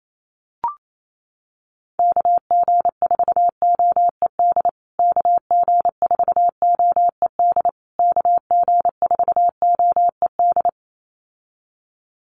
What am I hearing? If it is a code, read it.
Morse code, heard as a keyed tone, about 28 wpm, 700 Hz. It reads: KG4OED KG4OED KG4OED
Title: USA Call Signs Repeated 3x 28wpm